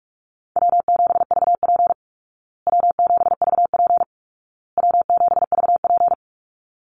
Morse code, heard as a keyed tone, about 44 wpm, 700 Hz. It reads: W7VP W7VP W7VP
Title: Call Signs Repeated 2x - 45wpm